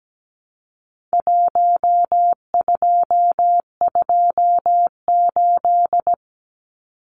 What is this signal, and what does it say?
Morse code, heard as a keyed tone, about 17 wpm, 700 Hz. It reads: 1228
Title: Numbers - Contest Style - 4-Digit 17wpm